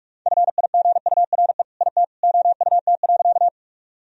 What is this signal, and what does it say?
Morse code, heard as a keyed tone, about 46 wpm, 700 Hz. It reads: FIGURE IT OUT.
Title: Sentences from Top 1000 Words - Review Part 2 - 45wpm